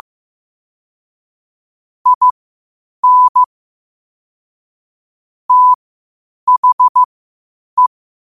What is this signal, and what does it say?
Morse code, heard as a keyed tone, about 15 wpm, spaced as for 8 wpm, 1000 Hz. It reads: IN THE
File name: qotdfiles / 8wpm